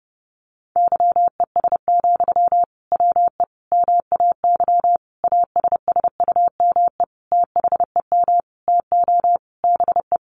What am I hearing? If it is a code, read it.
Morse code, heard as a keyed tone, about 30 wpm, 700 Hz. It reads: YES, WE MAY ASSUME THEM TO BE
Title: Sentences from Top 1500 Words 30wpm